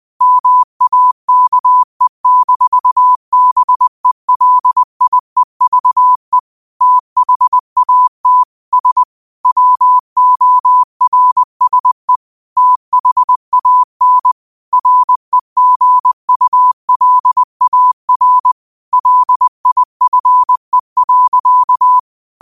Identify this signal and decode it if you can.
Morse code, heard as a keyed tone, about 20 wpm, 1000 Hz. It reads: MAKE-BELIEVE THAT S WORSE THAN REGULAR LIFE.
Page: New quotes every day in morse code at 20 Words per minute.